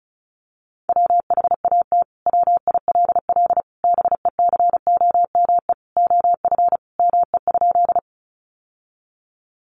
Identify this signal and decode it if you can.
Morse code, heard as a keyed tone, about 34 wpm, 700 Hz. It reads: WHAT WILL BECOME OF ME?
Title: Sentences from Top 600 Words 35wpm